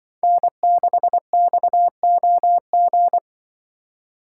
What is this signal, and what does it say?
Morse code, heard as a keyed tone, about 25 wpm, 700 Hz. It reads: N6XOG
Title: Speed Racing - USA 1x3 Call Signs 20wpm